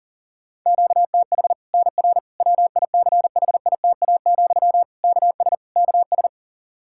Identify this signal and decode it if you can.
Morse code, heard as a keyed tone, about 40 wpm, 700 Hz. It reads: QTH NR WICHITA, KS KS
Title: Practice QSO - Exchange 1 - Beginning - 40wpm